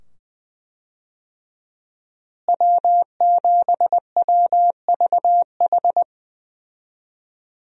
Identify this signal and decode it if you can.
Morse code, heard as a keyed tone, about 20 wpm, 700 Hz. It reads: W7WVH
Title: USA Call Signs - 5 Characters - Repeated 2x - 20wpm